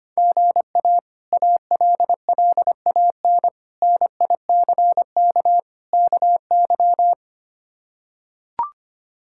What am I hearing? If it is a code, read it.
Morse code, heard as a keyed tone, about 25 wpm, 700 Hz. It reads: GA ALLAN NICK KY
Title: K1USN SST QSO Practice - 25wpm